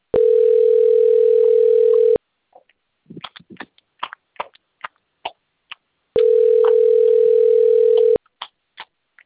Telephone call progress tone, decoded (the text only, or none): ringback tone